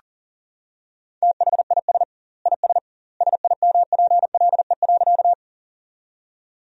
Morse code, heard as a keyed tone, about 40 wpm, 700 Hz. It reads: THIS IS SIMPLE.
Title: 2x Word-Spacing - Sentences from Top 400 Words - 40wpm